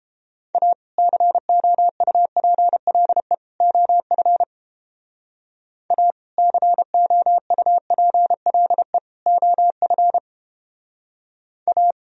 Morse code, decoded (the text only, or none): A COUPLE OF A COUPLE OF A